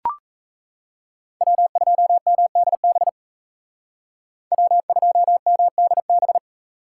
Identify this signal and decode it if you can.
Morse code, heard as a keyed tone, about 36 wpm, 700 Hz. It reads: W2MDB W2MDB